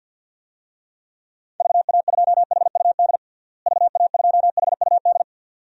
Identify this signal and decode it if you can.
Morse code, heard as a keyed tone, about 47 wpm, 700 Hz. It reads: VA2HUD VA2HUD
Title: DXCC-100 Call Signs - 6 Characters - Repeated 2x - 50wpm